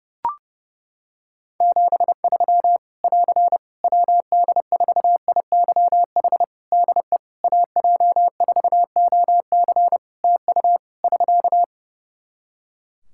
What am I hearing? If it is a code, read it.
Morse code, heard as a keyed tone, about 30 wpm, 700 Hz. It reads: 73 <AR> WD4IYH DE AJ4OC TU <SK>
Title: Practice QSO - Exchange 4 - Ending - 30wpm